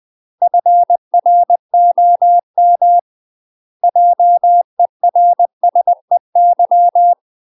Morse code, heard as a keyed tone, about 20 wpm, 700 Hz. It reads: FROM JERSEY
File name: UK MASTER QSOs TRAINING / 20WPM / UK MASTER QSOs 0048